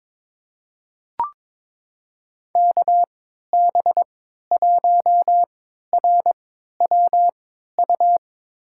Morse code, heard as a keyed tone, about 22 wpm, spaced as for 13 wpm, 700 Hz. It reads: KB1RWU
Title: USA Call Signs - 6 Characters - 3x Character Spacing - 22wpm